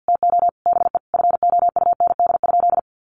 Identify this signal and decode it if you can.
Morse code, heard as a keyed tone, about 50 wpm, 700 Hz. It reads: TO BE FOUND?
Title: Sentences from Top 300 Words 50wpm